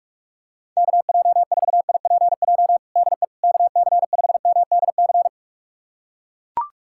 Morse code, heard as a keyed tone, about 42 wpm, 700 Hz. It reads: KJ4IPJ DE KC5MDC
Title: USA Callsign DE Callsign - 6 Characters - 45wpm